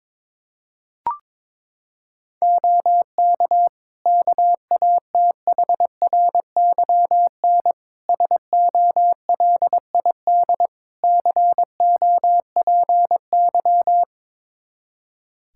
Morse code, heard as a keyed tone, about 22 wpm, 700 Hz. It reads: OK KATHRYN SOLID COPY
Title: K1USN SST QSO Practice 22wpm